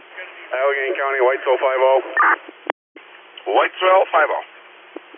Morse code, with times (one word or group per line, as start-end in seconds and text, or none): none